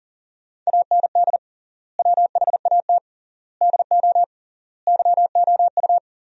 Morse code, heard as a keyed tone, about 37 wpm, 700 Hz. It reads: AND WHAT DO YOU
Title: 3x Word-Spacing - Sentences from Top 100 Words - 40wpm